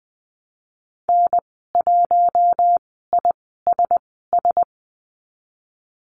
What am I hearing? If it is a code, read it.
Morse code, heard as a keyed tone, about 20 wpm, spaced as for 15 wpm, 700 Hz. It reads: N1ISS
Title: 055 Lesson 10g - Letter r - Call Signs 20wpm